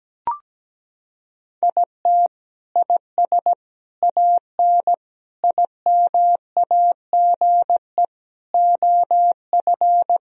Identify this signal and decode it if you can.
Morse code, heard as a keyed tone, about 17 wpm, 700 Hz. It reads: IT IS AN IMAGE OF
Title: Sentences from Top 1000 Words - Review Part 4 - 17wpm